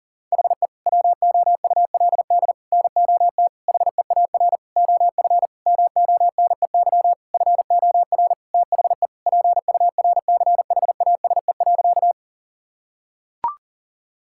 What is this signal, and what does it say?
Morse code, heard as a keyed tone, about 37 wpm, 690 Hz. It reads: HE WOULD NOT HEAR OF MONEY FOR THE PURCHASE.
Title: Sentences from Top 1000 Words - Review Part 5 - 40wpm